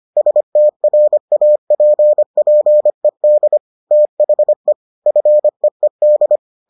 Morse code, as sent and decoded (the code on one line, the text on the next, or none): ... - .-. .- .--. .--. -.. / - .... / ..-. . -..
STRAPPED THE FEED